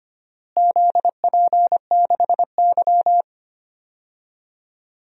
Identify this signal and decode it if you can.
Morse code, heard as a keyed tone, about 25 wpm, 700 Hz. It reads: ZP6Y
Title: DXCC-100 Call Signs - 4 Characters - Repeated 2x - 25wpm